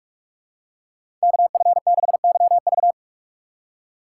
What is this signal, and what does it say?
Morse code, heard as a keyed tone, about 46 wpm, 700 Hz. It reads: KF6YV
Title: USA Call Signs - 5 Characters - 2x Character Spacing - 45wpm